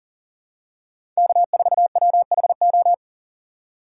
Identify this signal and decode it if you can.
Morse code, heard as a keyed tone, about 40 wpm, 700 Hz. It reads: K4WHO
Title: USA Call Signs - 4x Character Spacing - 40wpm